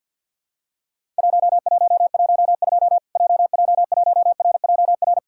Morse code, heard as a keyed tone, about 50 wpm, 700 Hz. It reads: 1112 JJ1RJR
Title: Worked All Europe DX Contest - QTC Exchange - 50wpm